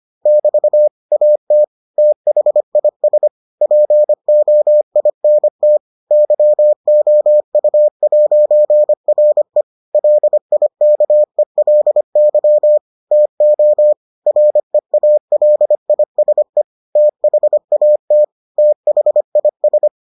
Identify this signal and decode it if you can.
Morse code, heard as a keyed tone, about 25 wpm, 600 Hz. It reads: <BT> AT THIS POINT YOU'RE LIKELY TO REALISE THAT THIS